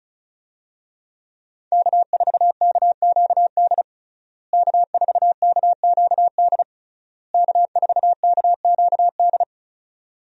Morse code, heard as a keyed tone, about 36 wpm, 700 Hz. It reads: K4KQD K4KQD K4KQD
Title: USA Call Signs - 5 Characters - Repeated 3x - 35wpm